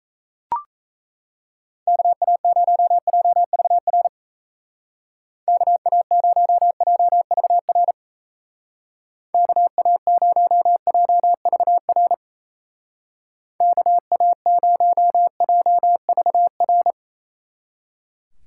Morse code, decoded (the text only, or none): KA0JVR KA0JVR KA0JVR KA0JVR